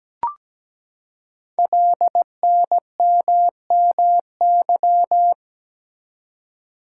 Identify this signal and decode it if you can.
Morse code, heard as a keyed tone, about 17 wpm, 700 Hz. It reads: LNMMY
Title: Code Groups - 5 Letters - 17wpm